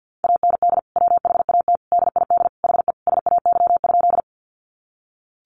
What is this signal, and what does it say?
Morse code, heard as a keyed tone, about 48 wpm, 700 Hz. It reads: AND WHAT DID HE SAY?
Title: Sentences from Top 200 Words 50wpm